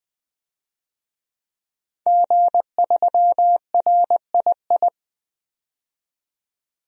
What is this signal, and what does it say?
Morse code, heard as a keyed tone, about 21 wpm, 700 Hz. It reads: G3RII
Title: DXCC-100 Call Signs - 5 Characters - 20wpm